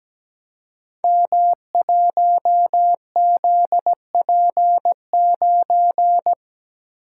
Morse code, heard as a keyed tone, about 17 wpm, 700 Hz. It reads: M1ZP9
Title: Code Groups - 5 Letter:Numbers - 17wpm